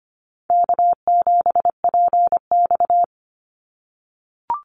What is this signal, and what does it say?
Morse code, heard as a keyed tone, about 24 wpm, 700 Hz. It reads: K7PX
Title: NA SOTA Chaser Callsigns 25wpm